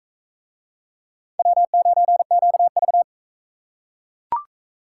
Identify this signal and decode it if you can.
Morse code, heard as a keyed tone, about 36 wpm, 700 Hz. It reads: W9QV